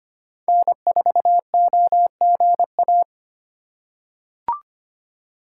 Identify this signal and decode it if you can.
Morse code, heard as a keyed tone, about 25 wpm, 700 Hz. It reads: N4OGA
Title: USA Call Signs - 5 Characters - 4x Character Spacing - 25wpm